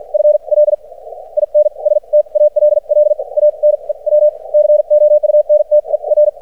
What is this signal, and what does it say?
Morse code, heard as a keyed tone, about 53 wpm, spaced as for 40 wpm, 600 Hz. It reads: FL INSTALL A NEW COUNTER
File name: archived / CRSnet